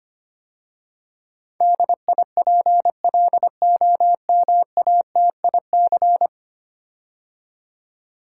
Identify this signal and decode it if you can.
Morse code, heard as a keyed tone, about 25 wpm, 700 Hz. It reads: DIPLOMATIC
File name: 10 Letter Words - First 3 Letters Given - 25wpm